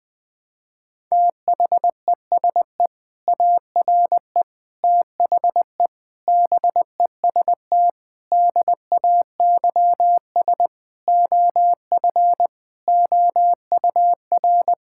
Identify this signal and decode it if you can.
Morse code, heard as a keyed tone, about 20 wpm, 700 Hz. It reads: THESE ARE THE BEST DAYS OF OUR
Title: Sentences with 2-Word Phrases - 20wpm